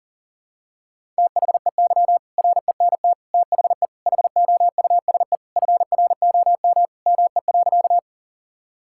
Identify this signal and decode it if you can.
Morse code, heard as a keyed tone, about 37 wpm, 700 Hz. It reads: THEY RENT THE HOUSE FROM ME.
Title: Sentences from Top 1000 Words - Review Part 5 - 40wpm